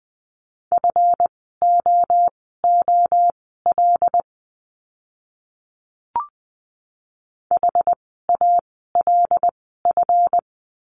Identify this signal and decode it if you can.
Morse code, heard as a keyed tone, about 20 wpm, spaced as for 15 wpm, 700 Hz. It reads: FOOL HALF
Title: 139 Lesson 23f - Letter f - Words 20wpm